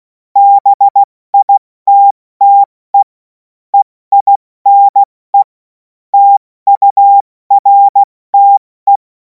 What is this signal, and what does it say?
Morse code, heard as a keyed tone, about 16 wpm, spaced as for 14 wpm, 800 Hz. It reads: BITTE EINE TURTE